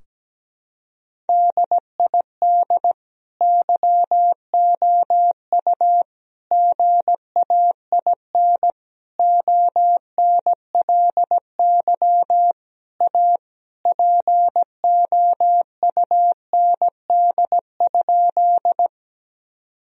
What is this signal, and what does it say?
Morse code, heard as a keyed tone, about 17 wpm, 700 Hz. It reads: DID YOU GAIN ONLY A POUND?
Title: Sentences from Top 1000 Words - Review Part 2 - 17wpm